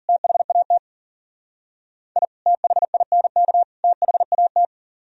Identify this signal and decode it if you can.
Morse code, heard as a keyed tone, about 52 wpm, 700 Hz. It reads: THAT I THINK THAT